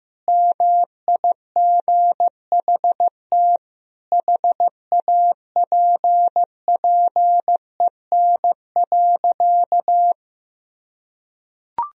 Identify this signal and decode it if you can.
Morse code, heard as a keyed tone, about 15 wpm, 700 Hz. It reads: MIGHT HAPPEN.
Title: Sentences from Top 400 Words and Abbreviations 15wpm